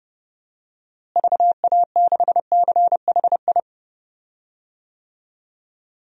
Morse code, heard as a keyed tone, about 30 wpm, 700 Hz. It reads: VA6CHI
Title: DXCC-100 Call Signs - 6 Characters - Repeated 2x - 30wpm